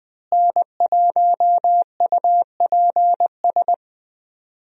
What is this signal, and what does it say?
Morse code, heard as a keyed tone, about 21 wpm, 700 Hz. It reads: N1UPS
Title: USA Call Signs - 5 Characters - Repeated 3x - 20wpm